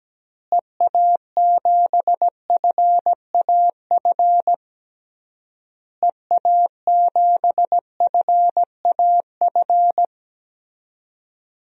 Morse code, heard as a keyed tone, about 17 wpm, 700 Hz. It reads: EA7FAF EA7FAF
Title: DXCC-100 Call Signs - 6 Characters - Repeated 3x - 17wpm